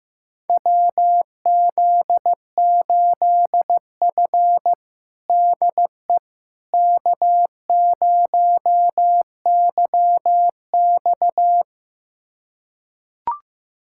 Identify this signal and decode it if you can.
Morse code, heard as a keyed tone, about 15 wpm, 700 Hz. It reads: WZ8F DE K0YX
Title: USA Callsign DE Callsign - 4 Characters - 15wpm